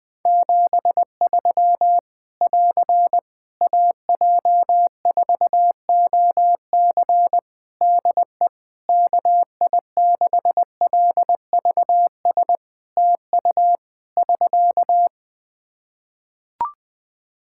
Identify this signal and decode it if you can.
Morse code, heard as a keyed tone, about 20 wpm, 700 Hz. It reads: 73 <AR> AJ4OC DE KI6LVS TU <SK>
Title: Practice QSO - Exchange 4 - Ending - 20wpm